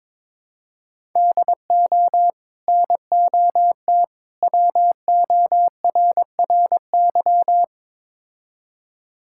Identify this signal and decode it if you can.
Morse code, heard as a keyed tone, about 22 wpm, 700 Hz. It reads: DO NOT WORRY
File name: practice / 22wpm / Build A Phrase - 5-Word Phrases - 22wpm